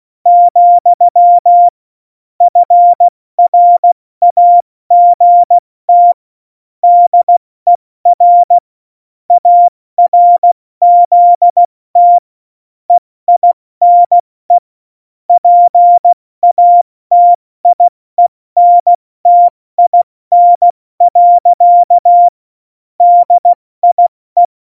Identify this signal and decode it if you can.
Morse code, heard as a keyed tone, about 16 wpm, 700 Hz. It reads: , FRAGT DER ARZT EINE PATIENTIN. DIE
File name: Morsekurs / Witze / Witz 15. September 2020, 70 F80